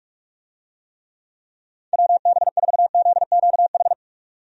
Morse code, heard as a keyed tone, about 42 wpm, 700 Hz. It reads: WB4ZQH
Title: USA Call Signs - 6 Characters - 45wpm